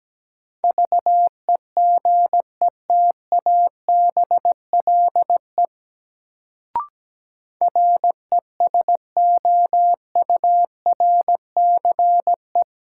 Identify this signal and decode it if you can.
Morse code, heard as a keyed tone, about 17 wpm, 700 Hz. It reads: VEGETABLE RESOURCE
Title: Sentences from Top 1300 Words - New Word Drill Alt - 17wpm